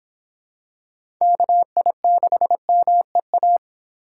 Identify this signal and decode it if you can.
Morse code, heard as a keyed tone, about 26 wpm, 700 Hz. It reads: KI6MEA